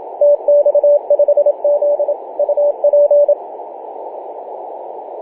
Morse code, heard as a keyed tone, about 26 wpm, 580 Hz, with radio noise. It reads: TX5Z UP
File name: posts / oct-08-2033z-tx5z-21-008mhz-cw